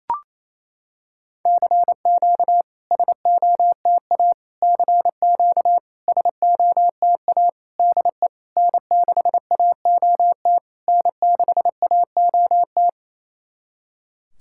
Morse code, heard as a keyed tone, about 28 wpm, 700 Hz. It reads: CQ SOTA CQ SOTA DE N6AOT N6AOT
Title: SOTA QSO Practice 28wpm